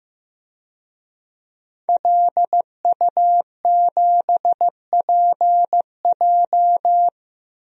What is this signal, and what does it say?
Morse code, heard as a keyed tone, about 15 wpm, 700 Hz. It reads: LU7PJ
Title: DXCC-100 Call Signs - 5 Characters - Repeated 3x - 15wpm